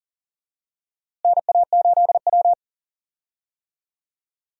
Morse code, heard as a keyed tone, about 40 wpm, 700 Hz. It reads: NA8W
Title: USA Call Signs - 4 Characters - Repeated 3x - 40wpm